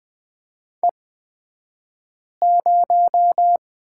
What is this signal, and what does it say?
Morse code, heard as a keyed tone, about 20 wpm, spaced as for 5 wpm, 700 Hz. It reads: E0
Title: 235 Lesson 38d - Letter q - Two Characters 20wpm